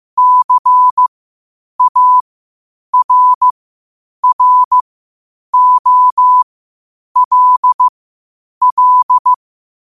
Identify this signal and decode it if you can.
Morse code, heard as a keyed tone, about 15 wpm, spaced as for 8 wpm, 1000 Hz. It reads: CARROLL
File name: qotdfiles / 8wpm